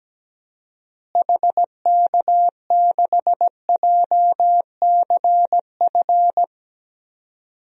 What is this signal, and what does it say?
Morse code, heard as a keyed tone, about 17 wpm, 700 Hz. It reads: HK6JCF
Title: DXCC-100 Call Signs - 6 Characters - Repeated 2x - 17wpm